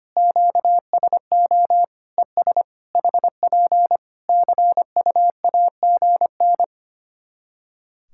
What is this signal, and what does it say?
Morse code, heard as a keyed tone, about 25 wpm, 700 Hz. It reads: QSO ES HP CUAGN
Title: Practice QSO - Exchange 4 - Ending - 25wpm